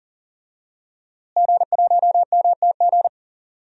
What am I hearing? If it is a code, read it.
Morse code, heard as a keyed tone, about 40 wpm, 700 Hz. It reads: G1MTG
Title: DXCC-100 Call Signs - 5 Characters - Repeated 2x - 40wpm